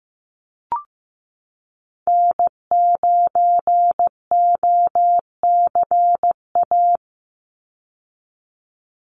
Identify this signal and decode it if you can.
Morse code, heard as a keyed tone, about 15 wpm, 700 Hz. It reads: N9OCA
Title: USA Call Signs 15wpm